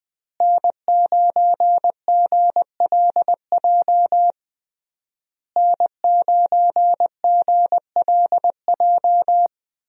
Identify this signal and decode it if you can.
Morse code, heard as a keyed tone, about 20 wpm, 700 Hz. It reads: N9GLJ N9GLJ
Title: USA Call Signs - 5 Characters - Repeated 3x - 20wpm